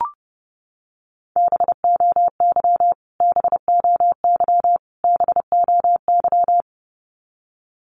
Morse code, heard as a keyed tone, about 30 wpm, 700 Hz. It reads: BOY BOY BOY
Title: Top 500 Words Repeated 30wpm